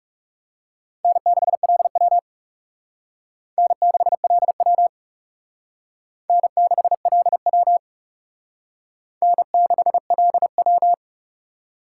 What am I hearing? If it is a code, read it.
Morse code, heard as a keyed tone, about 38 wpm, 700 Hz. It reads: N6LW N6LW N6LW N6LW